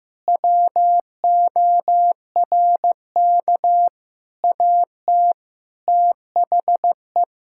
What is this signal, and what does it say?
Morse code, heard as a keyed tone, about 16 wpm, 700 Hz. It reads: WORK AT THE
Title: Sentences from Top 1000 Words - Review Part 5 - 15wpm